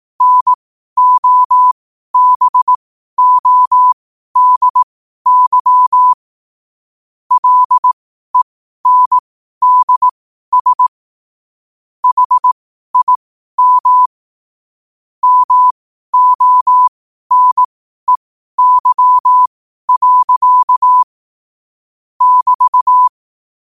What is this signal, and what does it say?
Morse code, heard as a keyed tone, about 18 wpm, spaced as for 12 wpm, 1000 Hz. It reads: NOBODY LENDS HIM MONEY. <BT>
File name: qotdfiles / 12wpm